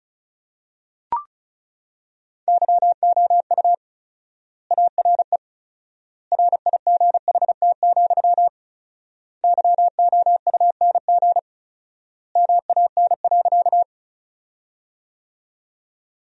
Morse code, heard as a keyed tone, about 34 wpm, 700 Hz. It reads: YOU ARE RIGHT, YOUNG MAN.
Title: 4x Word-Spacing - Sentences from Top 400 Words - 35wpm